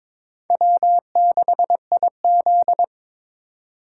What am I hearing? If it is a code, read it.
Morse code, heard as a keyed tone, about 21 wpm, 700 Hz. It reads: W6IZ
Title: USA Call Signs - 3x Character Spacing - 22wpm